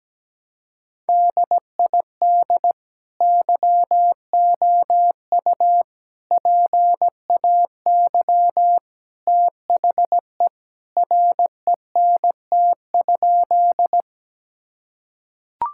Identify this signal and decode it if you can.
Morse code, heard as a keyed tone, about 17 wpm, 700 Hz. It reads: DID YOU PAY THE RENT?
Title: Sentences from Top 1000 Words - Review Part 5 - 17wpm